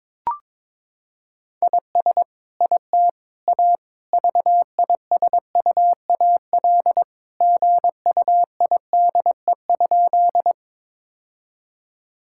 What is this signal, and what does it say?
Morse code, heard as a keyed tone, about 22 wpm, 700 Hz. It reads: IS IT A VISUAL GUIDE?
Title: Sentences from Top 1000 Words - Review Part 6 - 22wpm